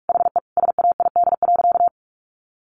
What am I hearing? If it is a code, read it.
Morse code, heard as a keyed tone, about 44 wpm, 700 Hz. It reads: HE SAID.
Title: Sentences from Top 1300 Words 45wpm